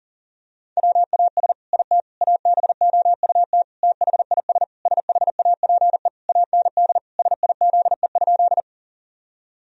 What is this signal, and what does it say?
Morse code, heard as a keyed tone, about 40 wpm, 700 Hz. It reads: WAS IT ABOUT THIS SHAPE AND SIZE?
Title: Sentences from Top 1000 Words - Review Part 3 - 40wpm